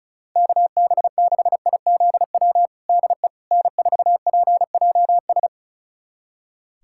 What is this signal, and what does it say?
Morse code, heard as a keyed tone, about 34 wpm, 700 Hz. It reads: KB6IZW DE N4PJS
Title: USA Callsign DE Callsign 35wpm